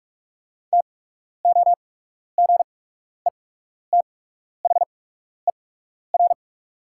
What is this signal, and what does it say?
Morse code, heard as a keyed tone, about 46 wpm, spaced as for 12 wpm, 700 Hz. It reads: TOGETHER
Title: 8x Character Spacing - Top 300 Words - 45wpm